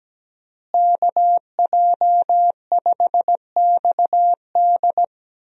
Morse code, heard as a keyed tone, about 17 wpm, 700 Hz. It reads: KJ5XD
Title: USA Call Signs - 5 Characters - Repeated 2x - 17wpm